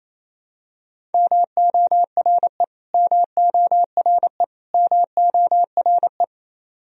Morse code, heard as a keyed tone, about 28 wpm, 700 Hz. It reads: MORE MORE MORE
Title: Top 100 Words Repeated 28wpm